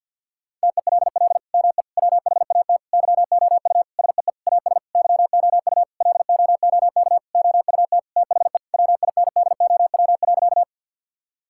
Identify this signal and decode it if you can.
Morse code, heard as a keyed tone, about 50 wpm, 700 Hz. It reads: TELL ME WHAT YOU SEE AS YOU LOOK OUT THE WINDOW.
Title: Sentences from Top 1600 Words 50wpm